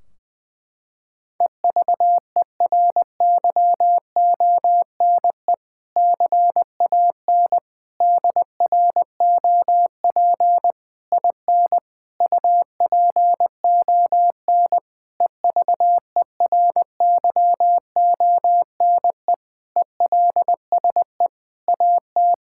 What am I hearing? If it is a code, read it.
Morse code, heard as a keyed tone, about 20 wpm, 700 Hz. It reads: EVERYONE CAN DROP IN UPON EVERYONE ELSE AT
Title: Sentences from Top 1600 Words 20wpm